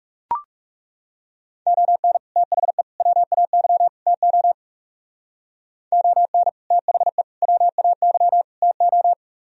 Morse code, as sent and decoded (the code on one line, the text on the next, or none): --- -. / - .... / .-- .- -.-- / - --- / --- -. / - .... / .-- .- -.-- / - ---
ON THE WAY TO ON THE WAY TO